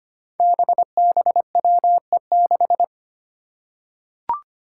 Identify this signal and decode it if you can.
Morse code, heard as a keyed tone, about 25 wpm, 700 Hz. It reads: BBWE6
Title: Five Challenging Letter-Number Combinations 25wpm